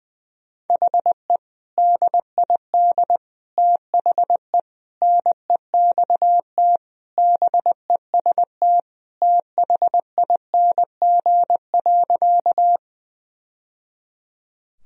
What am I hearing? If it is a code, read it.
Morse code, heard as a keyed tone, about 20 wpm, 700 Hz. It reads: HE DID THE NEXT BEST THING.
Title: Sentences from Top 1000 Words - Review Part 1 - 20wpm